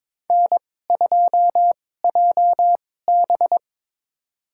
Morse code, heard as a keyed tone, about 21 wpm, spaced as for 16 wpm, 700 Hz. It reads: N2JB
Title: USA Call Signs - 2x Character Spacing - 22wpm